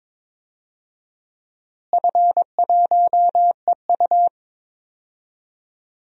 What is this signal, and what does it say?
Morse code, heard as a keyed tone, about 21 wpm, 700 Hz. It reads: F1EU
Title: DXCC-100 Call Signs - 4 Characters - Repeated 2x - 22wpm